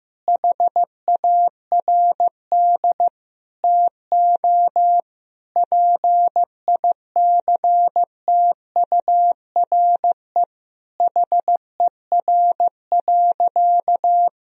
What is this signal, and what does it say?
Morse code, heard as a keyed tone, about 15 wpm, 700 Hz. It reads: HARD TO PICTURE HER.
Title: Sentences from Top 1000 Words - Review Part 1 - 15wpm